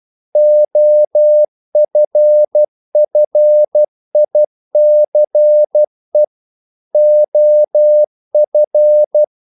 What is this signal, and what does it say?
Morse code, heard as a keyed tone, about 12 wpm, 600 Hz. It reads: OFFICE OF
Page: English quotations in CW (telegraphy - morse)